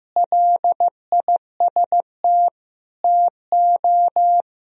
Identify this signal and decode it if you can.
Morse code, heard as a keyed tone, about 15 wpm, 700 Hz. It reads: LIST TO
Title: Sentences from Top 1000 Words - Review Part 2 - 15wpm